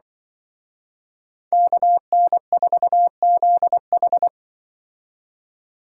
Morse code, heard as a keyed tone, about 23 wpm, 700 Hz. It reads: KN4ZH
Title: Speed Racing - USA 2x2 Call Signs 20wpm